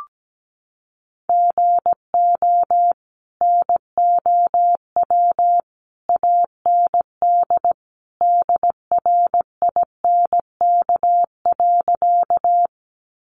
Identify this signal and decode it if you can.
Morse code, heard as a keyed tone, about 17 wpm, 700 Hz. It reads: GO NOW AND DRINK.
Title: Sentences from Top 1500 Words 17wpm